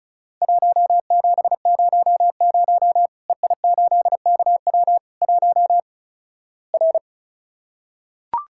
Morse code, heard as a keyed tone, about 36 wpm, 700 Hz. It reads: 1700 EI8KW 1 R
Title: Worked All Europe DX Contest - QTC Exchange - 35wpm